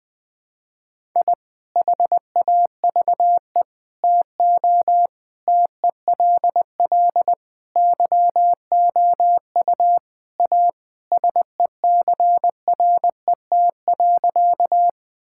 Morse code, heard as a keyed tone, about 21 wpm, 700 Hz. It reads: I HAVE TO TELL YOU A SECRET.
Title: Sentences with 3-Word Phrases - 20wpm